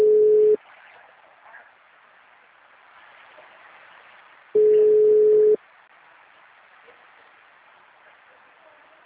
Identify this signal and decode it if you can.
Ringback tone (Europe, 425 Hz).
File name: out- -2026